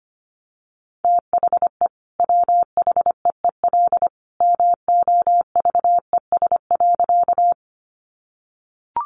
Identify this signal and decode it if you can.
Morse code, heard as a keyed tone, about 25 wpm, 700 Hz. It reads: THE WHEEL MOVES.
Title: Additional Easy Phrases 25wpm